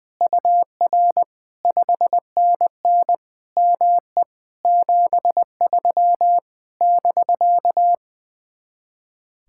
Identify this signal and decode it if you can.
Morse code, heard as a keyed tone, about 20 wpm, 700 Hz. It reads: UR 5NN ME 73 <BK>
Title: Speed-Racing Common QSO Phrases - 20wpm